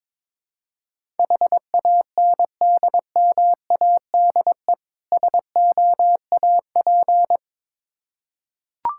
Morse code, heard as a keyed tone, about 22 wpm, 700 Hz. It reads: HANDMADE SOAP
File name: Difficult Words with Hints - 22wpm